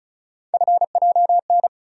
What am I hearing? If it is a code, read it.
Morse code, heard as a keyed tone, about 36 wpm, 700 Hz. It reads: FJN
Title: USA Call Sign Postfixes - 3 Letters 35wpm